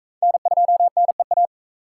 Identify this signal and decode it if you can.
Morse code, heard as a keyed tone, about 40 wpm, 700 Hz. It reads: N2NEA